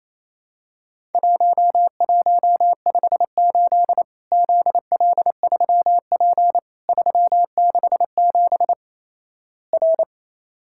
Morse code, heard as a keyed tone, about 28 wpm, 700 Hz. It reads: 1158 ZL3P 367 R
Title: Worked All Europe DX Contest - QTC Exchange - 28wpm